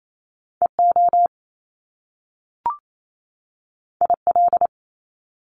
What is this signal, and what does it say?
Morse code, heard as a keyed tone, about 28 wpm, 700 Hz. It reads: EO IL
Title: Top Two Letter Combinations in Words 28wpm